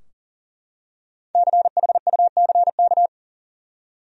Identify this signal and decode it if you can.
Morse code, heard as a keyed tone, about 40 wpm, 700 Hz. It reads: CHUCK
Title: USA Common Names - Max 5 Letters - 40wpm